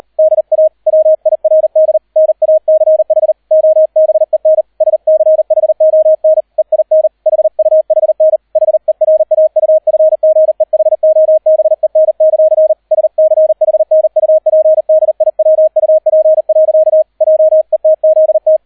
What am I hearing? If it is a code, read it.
Morse code, heard as a keyed tone, about 38 wpm, 630 Hz, with radio noise. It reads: DA WIRD NACH OBEN SCHON EIN HUHN HERAUFGEHOBEN; SCHNUPDIWUP. JETZT